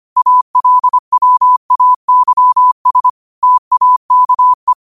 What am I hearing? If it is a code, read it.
Morse code, heard as a keyed tone, about 25 wpm, 1000 Hz. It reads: ALWAYS TAKE
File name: qotdfiles / 25wpm